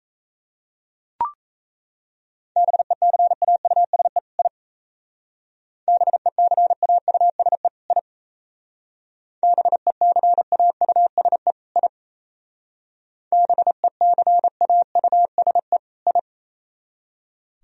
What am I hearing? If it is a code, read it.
Morse code, heard as a keyed tone, about 33 wpm, 690 Hz. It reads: BECAUSE I BECAUSE I BECAUSE I BECAUSE I
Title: Speed-Racing - Sets of 2 Words 28wpm